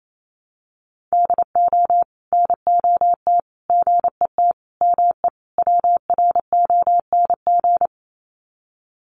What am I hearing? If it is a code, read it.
Morse code, heard as a keyed tone, about 28 wpm, 700 Hz. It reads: DO NOT GET ME WRONG
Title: Sets of 5 Words Encore 28wpm